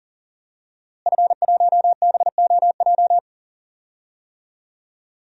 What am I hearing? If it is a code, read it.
Morse code, heard as a keyed tone, about 38 wpm, 700 Hz. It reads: F1BOJ
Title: DXCC-100 Call Signs - 5 Characters - Repeated 3x - 40wpm